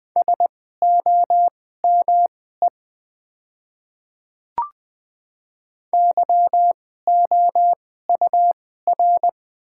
Morse code, heard as a keyed tone, about 20 wpm, spaced as for 15 wpm, 700 Hz. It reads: SOME YOUR
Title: 2x Character Spacing - Top 100 Words - 20wpm